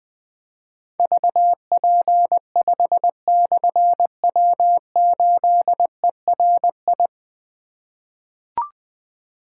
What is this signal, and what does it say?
Morse code, heard as a keyed tone, about 20 wpm, 700 Hz. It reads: VP5/W8ERI
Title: Difficult Call Signs 20wpm